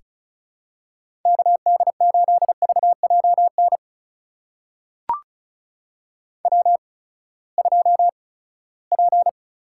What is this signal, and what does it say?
Morse code, heard as a keyed tone, about 36 wpm, spaced as for 11 wpm, 700 Hz. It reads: KD8VJ NW2P
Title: USA Call Signs - 8x Character Spacing - 35wpm